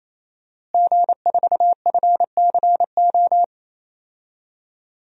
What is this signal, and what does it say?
Morse code, heard as a keyed tone, about 28 wpm, 700 Hz. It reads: G4FCO
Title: DXCC-100 Call Signs - 5 Characters - Repeated 3x - 28wpm